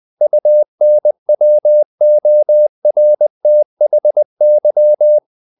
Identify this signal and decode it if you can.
Morse code, heard as a keyed tone, about 20 wpm, 600 Hz. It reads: UNWORTHY
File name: the old man and the sea / the old man and the sea part 1 20wpm